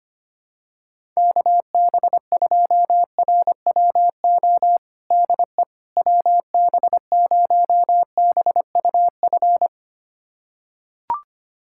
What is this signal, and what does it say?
Morse code, heard as a keyed tone, about 25 wpm, 700 Hz. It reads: KB2RWO DE WB0BUF
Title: USA Callsign DE Callsign - 6 Characters - 25wpm